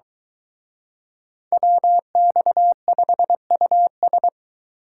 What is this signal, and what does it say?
Morse code, heard as a keyed tone, about 23 wpm, 700 Hz. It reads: WX5US